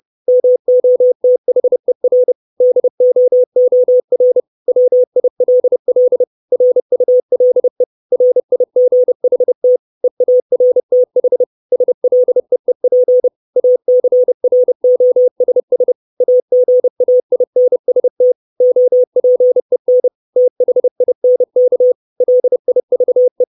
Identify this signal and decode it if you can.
Morse code, heard as a keyed tone, about 30 wpm, 500 Hz. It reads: MOTHER DOOR WILL RULE RIGHT EARTH SLEEP ACROSS AGAINST OPEN THINK LIVE
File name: 632 Words 30WPM